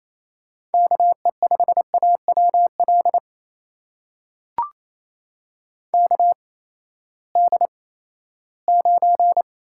Morse code, heard as a keyed tone, about 28 wpm, spaced as for 16 wpm, 700 Hz. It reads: KE5AWL KD9
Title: USA Call Signs - 6 Characters - 8x Character Spacing - 28wpm